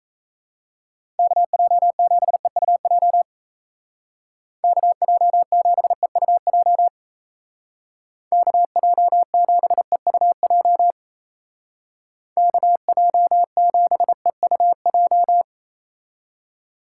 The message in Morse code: KJ7EUJ KJ7EUJ KJ7EUJ KJ7EUJ

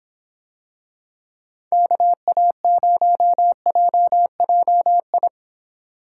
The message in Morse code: KA0JJI

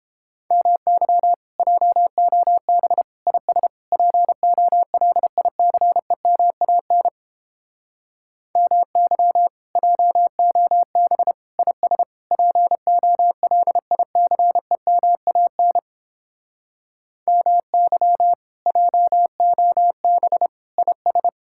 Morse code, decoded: MY JOB IS POLICEMAN MY JOB IS POLICEMAN MY JOB IS

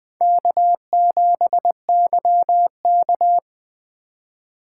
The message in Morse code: K7YK

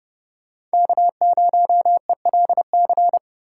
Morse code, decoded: K0ELC

